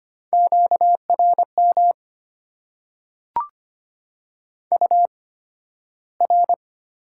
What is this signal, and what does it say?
Morse code, heard as a keyed tone, about 25 wpm, spaced as for 13 wpm, 700 Hz. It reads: QRM UR
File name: Top 90 QSO Elements - 8x Character Spacing - 25wpm